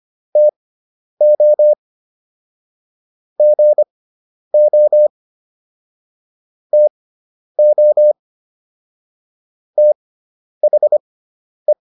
Morse code, heard as a keyed tone, about 25 wpm, spaced as for 10 wpm, 600 Hz. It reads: TO GO TO THE